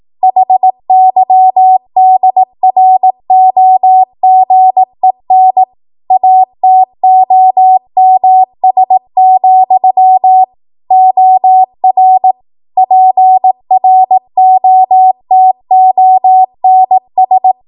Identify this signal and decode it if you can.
Morse code, heard as a keyed tone, about 18 wpm, 750 Hz. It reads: HYDROGEN ATOMS, OR PROTONS